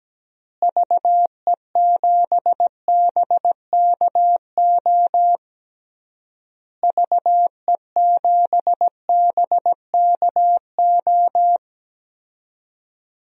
Morse code, heard as a keyed tone, about 17 wpm, 700 Hz. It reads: VE7BKO VE7BKO
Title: DXCC-100 Call Signs - 6 Characters - Repeated 3x - 17wpm